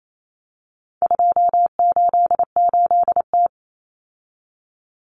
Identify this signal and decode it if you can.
Morse code, heard as a keyed tone, about 28 wpm, 700 Hz. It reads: 288T
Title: Common Cut Numbers - 4-Digit 28wpm